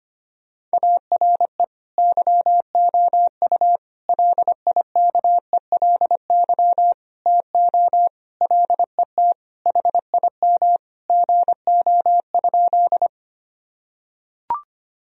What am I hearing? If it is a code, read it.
Morse code, heard as a keyed tone, about 25 wpm, 700 Hz. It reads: ARE YOU LIKELY TO LET HIM GO?
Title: Sentences from Top 1000 Words - Review Part 4 - 25wpm